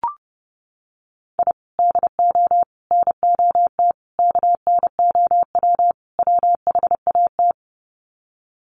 Morse code, heard as a keyed tone, about 30 wpm, 700 Hz. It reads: I DO NOT KNOW WHAT
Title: Sets of 5 Words 30wpm